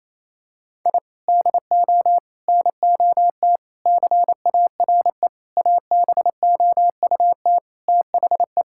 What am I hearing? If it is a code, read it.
Morse code, heard as a keyed tone, about 28 wpm, 700 Hz. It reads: I DO NOT CARE ABOUT THE
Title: Sentences from Top 1000 Words - Review Part 5 - 28wpm